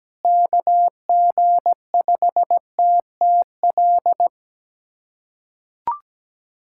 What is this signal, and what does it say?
Morse code, heard as a keyed tone, about 17 wpm, 700 Hz. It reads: KG5TTL
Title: USA Callsign DE Callsign 17wpm